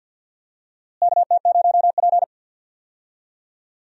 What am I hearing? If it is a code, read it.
Morse code, heard as a keyed tone, about 48 wpm, 700 Hz. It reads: KT0P
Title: USA Call Signs - 4 Characters - Repeated 3x - 50wpm